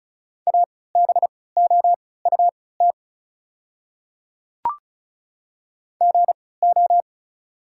Morse code, heard as a keyed tone, about 34 wpm, spaced as for 20 wpm, 700 Hz. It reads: ABOUT GO
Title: 3x Character Spacing - Top 100 Words - 35wpm